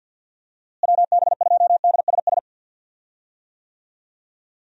Morse code, heard as a keyed tone, about 51 wpm, 700 Hz. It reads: WB2DSS
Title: USA Call Signs - 6 Characters - Repeated 2x - 50wpm